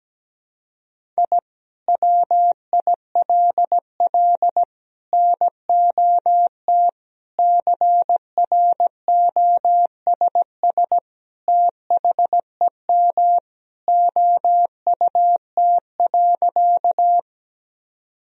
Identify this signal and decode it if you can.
Morse code, heard as a keyed tone, about 17 wpm, 700 Hz. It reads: I WILL NOT CROSS THEM OUT.
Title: Sentences from Top 1000 Words - Review Part 1 - 17wpm